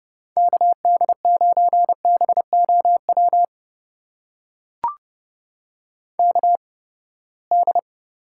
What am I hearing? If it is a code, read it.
Morse code, heard as a keyed tone, about 30 wpm, spaced as for 21 wpm, 700 Hz. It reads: KD9BOW KD
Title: USA Call Signs - 6 Characters - 8x Character Spacing - 30wpm